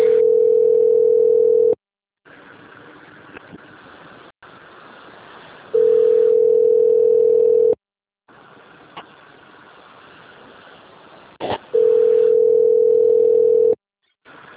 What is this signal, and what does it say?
Ringback tone (North America), 440 + 480 Hz.
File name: out- - - - - .12760